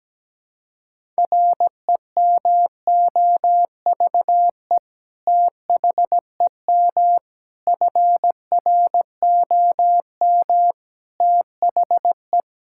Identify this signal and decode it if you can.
Morse code, heard as a keyed tone, about 17 wpm, 700 Hz. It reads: REMOVE THEM FROM THE
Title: Sentences from Top 1000 Words - Review Part 5 - 17wpm